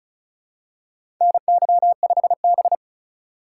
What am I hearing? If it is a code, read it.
Morse code, heard as a keyed tone, about 36 wpm, 700 Hz. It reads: NY5B
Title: USA Call Signs - 3x Character Spacing - 35wpm